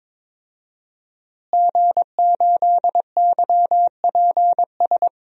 Morse code, heard as a keyed tone, about 22 wpm, 700 Hz. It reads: G8YPS
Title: DXCC-100 Call Signs - 5 Characters - Repeated 2x - 22wpm